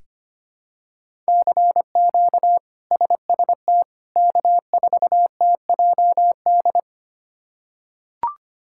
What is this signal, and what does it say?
Morse code, heard as a keyed tone, about 25 wpm, 700 Hz. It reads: CQ SST K4TJD
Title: K1USN SST QSO Practice - 25wpm